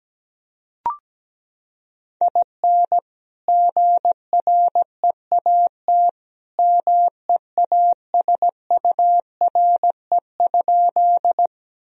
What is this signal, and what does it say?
Morse code, heard as a keyed tone, about 17 wpm, 700 Hz. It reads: IN GREAT MEASURE?
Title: Sentences from Top 1000 Words - Review Part 2 - 17wpm